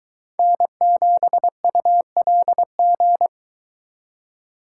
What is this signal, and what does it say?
Morse code, heard as a keyed tone, about 22 wpm, 700 Hz. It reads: N7ULG